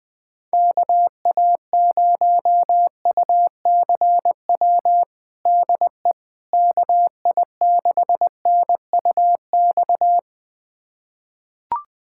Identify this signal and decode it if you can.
Morse code, heard as a keyed tone, about 20 wpm, 700 Hz. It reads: KA0UCW DE KI6NUX
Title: USA Callsign DE Callsign - 6 Characters - 20wpm